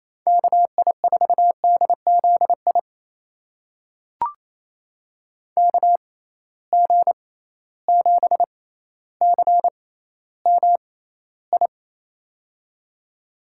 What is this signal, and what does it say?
Morse code, heard as a keyed tone, about 28 wpm, spaced as for 14 wpm, 700 Hz. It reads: KI4DZI KG7CMI
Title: USA Call Signs - 6 Characters - 6x Character Spacing - 28wpm